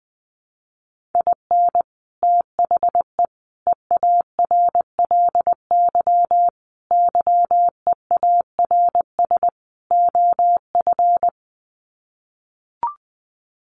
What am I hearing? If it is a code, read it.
Morse code, heard as a keyed tone, about 20 wpm, 700 Hz. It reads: IN THE EARLY YEARS OF
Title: Sets of 5 Words Encore 20wpm